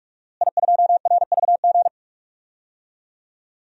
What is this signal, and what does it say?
Morse code, heard as a keyed tone, about 47 wpm, 700 Hz. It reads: I2RVG
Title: DXCC-100 Call Signs - 5 Characters - Repeated 3x - 45wpm